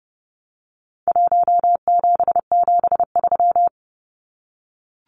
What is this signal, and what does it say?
Morse code, heard as a keyed tone, about 30 wpm, 700 Hz. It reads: 1773
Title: Numbers - Contest Style - 4-Digit 30wpm